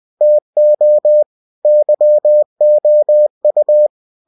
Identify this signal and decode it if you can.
Morse code, heard as a keyed tone, about 20 wpm, 600 Hz. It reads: TO YOU